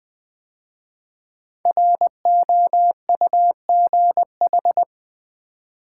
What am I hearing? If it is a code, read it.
Morse code, heard as a keyed tone, about 20 wpm, 700 Hz. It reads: ROUGH
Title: 5 Letter Words - First 3 Letters Given - 20wpm